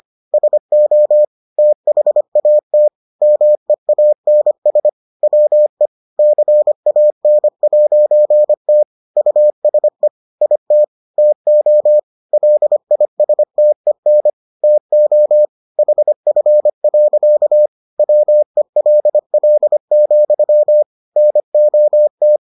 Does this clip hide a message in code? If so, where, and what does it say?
Yes: Morse code from 0.3 to 22.4 s, heard as a keyed tone, about 25 wpm, 600 Hz. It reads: SO THAT MEANS WE CAN'T USE IT TO LISTEN TO HF. WELL, NOT